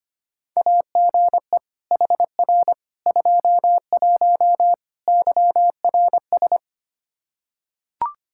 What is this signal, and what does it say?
Morse code, heard as a keyed tone, about 25 wpm, 700 Hz. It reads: AGE HR 21 YRS
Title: Practice QSO - Exchange 3 - More Info - 25wpm